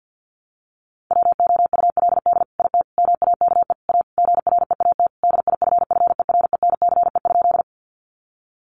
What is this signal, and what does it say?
Morse code, heard as a keyed tone, about 50 wpm, 700 Hz. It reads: WOULD IT MAKE A GREAT DIFFERENCE?
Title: Sentences from Top 800 Words 50wpm